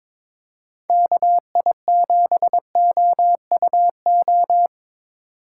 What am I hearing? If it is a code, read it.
Morse code, heard as a keyed tone, about 22 wpm, 700 Hz. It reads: KI7OUO
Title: USA Call Signs - 4x Character Spacing - 22wpm